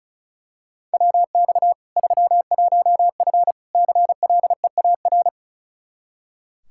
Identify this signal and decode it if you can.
Morse code, heard as a keyed tone, about 36 wpm, 700 Hz. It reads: WX 31F CLEAR
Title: Practice QSO - Exchange 2 - Info - 35wpm